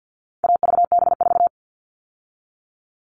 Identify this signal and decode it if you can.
Morse code, heard as a keyed tone, about 50 wpm, 700 Hz. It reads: AVB4
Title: All Cut Numbers - 4-Digit 50wpm